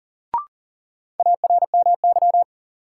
Morse code, heard as a keyed tone, about 39 wpm, 700 Hz. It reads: ARMY
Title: Sentences from Top 800 Words - New Word Drill - 40wpm